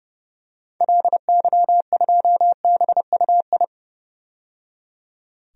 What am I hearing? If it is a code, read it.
Morse code, heard as a keyed tone, about 30 wpm, 700 Hz. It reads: LY2BUI
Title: DXCC-100 Call Signs - 6 Characters - 30wpm